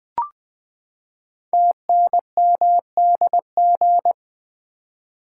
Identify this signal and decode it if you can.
Morse code, heard as a keyed tone, about 20 wpm, 700 Hz. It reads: TNMDG
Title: Code Groups - 5 Letters - 20wpm